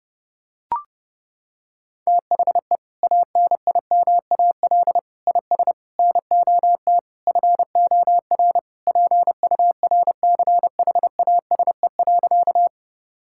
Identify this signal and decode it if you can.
Morse code, heard as a keyed tone, about 30 wpm, 700 Hz. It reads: THE ANIMAL IS NOT FOR PURCHASE.
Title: Sentences from Top 1000 Words - Review Part 5 - 30wpm